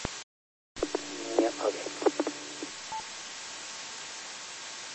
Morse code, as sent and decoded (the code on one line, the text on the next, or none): none